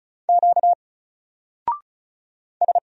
Morse code, heard as a keyed tone, about 36 wpm, 700 Hz. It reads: Q S
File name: Single-Letter-Number-35wpm